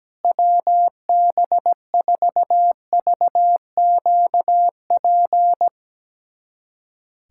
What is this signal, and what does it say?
Morse code, heard as a keyed tone, about 17 wpm, 700 Hz. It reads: WB4VQP
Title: USA Callsign DE Callsign - 6 Characters - 17wpm